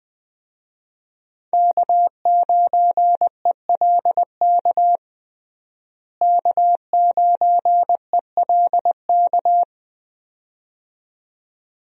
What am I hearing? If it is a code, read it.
Morse code, heard as a keyed tone, about 20 wpm, 700 Hz. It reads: K9ELK K9ELK
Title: USA Call Signs - 5 Characters - Repeated 2x - 20wpm